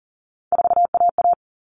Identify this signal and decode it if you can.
Morse code, heard as a keyed tone, about 40 wpm, 700 Hz. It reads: 4AA
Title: All Cut Numbers - 3-Digit 40wpm